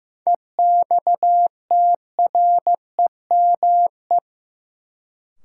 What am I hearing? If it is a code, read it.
Morse code, heard as a keyed tone, about 15 wpm, 700 Hz. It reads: EXTREME
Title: Sentences from Top 1300 Words - New Word Drill Alt - 15wpm